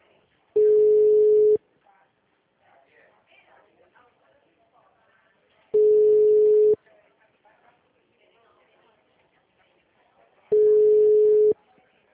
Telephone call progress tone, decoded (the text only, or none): ringback tone